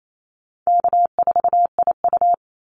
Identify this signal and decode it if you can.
Morse code, heard as a keyed tone, about 28 wpm, 700 Hz. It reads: K4IU
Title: Call Signs - 28wpm